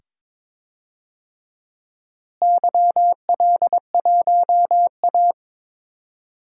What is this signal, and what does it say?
Morse code, heard as a keyed tone, about 22 wpm, 700 Hz. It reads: YL1A
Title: DXCC-100 Call Signs - 4 Characters - Repeated 2x - 22wpm